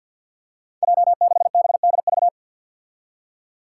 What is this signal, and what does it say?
Morse code, heard as a keyed tone, about 47 wpm, 700 Hz. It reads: J6BDV
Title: Five Challenging Letter-Number Combinations 50wpm